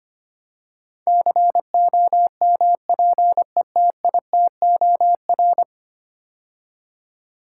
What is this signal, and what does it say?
Morse code, heard as a keyed tone, about 25 wpm, 700 Hz. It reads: COMPETITOR
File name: 10 Letter Words - First 3 Letters Given - 25wpm